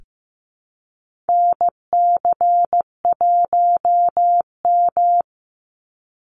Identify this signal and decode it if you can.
Morse code, heard as a keyed tone, about 15 wpm, 700 Hz. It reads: NC1M
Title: Call Signs Repeated 2x - 15wpm